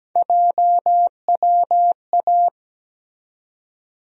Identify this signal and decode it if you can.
Morse code, heard as a keyed tone, about 17 wpm, 700 Hz. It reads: JWA
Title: USA Call Sign Postfixes - 3 Letters 17wpm